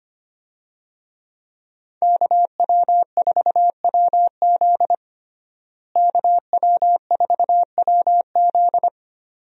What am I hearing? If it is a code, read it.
Morse code, heard as a keyed tone, about 25 wpm, 700 Hz. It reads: KW4WZ KW4WZ
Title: USA Call Signs - 5 Characters - Repeated 2x - 25wpm